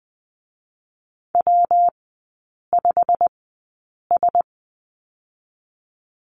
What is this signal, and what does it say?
Morse code, heard as a keyed tone, about 20 wpm, 700 Hz. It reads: W 5 S
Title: 172 Lesson 28e - Number 9 - Three Characters 20wpm